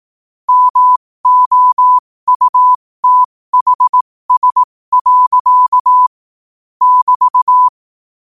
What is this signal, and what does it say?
Morse code, heard as a keyed tone, about 18 wpm, spaced as for 15 wpm, 1000 Hz. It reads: MOUTHS. <BT>
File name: qotdfiles / 15wpm